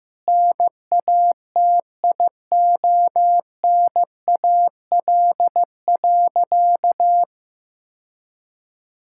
Morse code, heard as a keyed tone, about 15 wpm, 700 Hz. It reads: NATIONAL.
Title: Sentences from Top 1000 Words - Review Part 5 - 15wpm